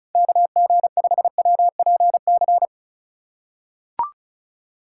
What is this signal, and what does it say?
Morse code, heard as a keyed tone, about 36 wpm, 700 Hz. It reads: KG5WPC
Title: USA Call Signs - 6 Characters - 4x Character Spacing - 35wpm